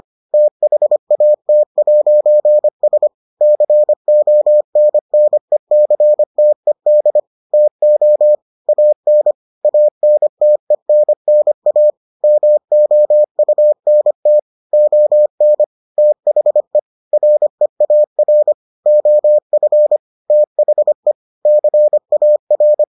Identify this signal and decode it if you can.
Morse code, heard as a keyed tone, about 25 wpm, 600 Hz. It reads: THAT'S CONNECTED TO AN ANTENNA MOUNT ON THE REAR OF THE CAR